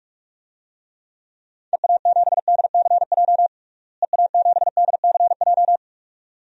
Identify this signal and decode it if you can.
Morse code, heard as a keyed tone, about 48 wpm, 700 Hz. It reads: EA7DCJ EA7DCJ
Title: DXCC-100 Call Signs - 6 Characters - Repeated 2x - 45wpm